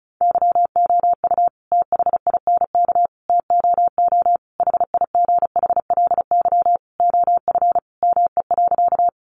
Morse code, heard as a keyed tone, about 34 wpm, 700 Hz. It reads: YOU THINK TOO HIGHLY OF ME.
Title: Sentences from Top 1100 Words 35wpm